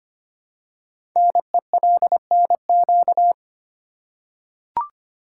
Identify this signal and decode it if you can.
Morse code, heard as a keyed tone, about 25 wpm, 700 Hz. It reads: NELNQ
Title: Code Groups - 5 Letters - 25wpm